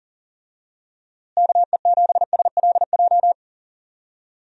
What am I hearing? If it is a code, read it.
Morse code, heard as a keyed tone, about 40 wpm, 700 Hz. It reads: KE7SLJ
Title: USA Call Signs - 6 Characters - Repeated 3x - 40wpm